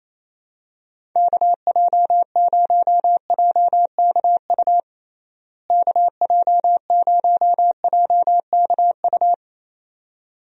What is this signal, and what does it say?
Morse code, heard as a keyed tone, about 28 wpm, 700 Hz. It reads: KJ0JKU KJ0JKU
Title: USA Call Signs - 6 Characters - Repeated 2x - 28wpm